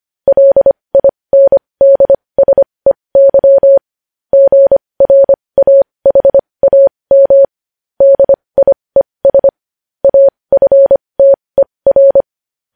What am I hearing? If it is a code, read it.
Morse code, heard as a keyed tone, about 25 wpm, spaced as for 20 wpm, 550 Hz. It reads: LINDSEY GRAHAM DIES AFTER
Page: news headlines in morse code at 20 wpm